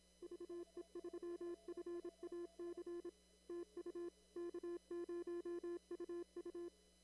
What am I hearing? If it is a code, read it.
Morse code, heard as a keyed tone, about 26 wpm, 340 Hz. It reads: VE3FAC TU K0UU
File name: CW / kt5j wpx cw-op-k5pi